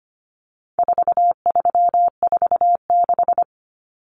Morse code, heard as a keyed tone, about 25 wpm, 700 Hz. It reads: 4346
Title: Numbers - 4-Digit 25wpm